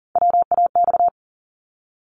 Morse code, heard as a keyed tone, about 40 wpm, 690 Hz. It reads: WAX
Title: Multi-letter Phonemes 40wpm